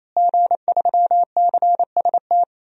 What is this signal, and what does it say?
Morse code, heard as a keyed tone, about 28 wpm, 700 Hz. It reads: G3CST
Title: DXCC-100 Call Signs - 5 Characters - Repeated 3x - 28wpm